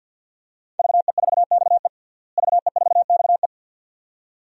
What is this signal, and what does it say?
Morse code, heard as a keyed tone, about 52 wpm, 700 Hz. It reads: VE4XE VE4XE
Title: DXCC-100 Call Signs - 5 Characters - Repeated 2x - 50wpm